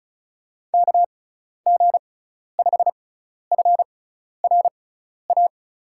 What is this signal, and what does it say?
Morse code, heard as a keyed tone, about 36 wpm, spaced as for 12 wpm, 690 Hz. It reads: KG5FRA
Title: USA Call Signs - 6 Characters - 6x Character Spacing - 35wpm